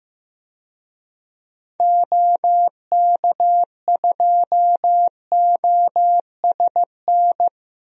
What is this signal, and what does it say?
Morse code, heard as a keyed tone, about 15 wpm, 700 Hz. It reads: OK2OSN
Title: DXCC-100 Call Signs - 6 Characters - Repeated 3x - 15wpm